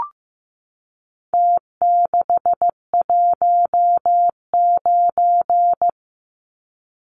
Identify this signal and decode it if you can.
Morse code, heard as a keyed tone, about 15 wpm, 700 Hz. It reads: T619
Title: Numbers - Contest Style - 4-Digit 15wpm